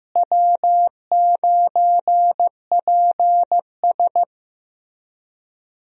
Morse code, heard as a keyed tone, about 15 wpm, 700 Hz. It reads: W9PS
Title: USA Call Signs - 4 Characters - Repeated 3x - 15wpm